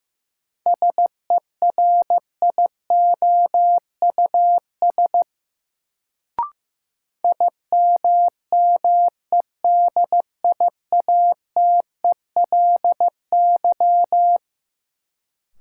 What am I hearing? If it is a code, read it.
Morse code, heard as a keyed tone, about 15 wpm, 700 Hz. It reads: SERIOUS IMMEDIATELY
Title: Sentences from Top 1100 Words - New Word Drill - 15wpm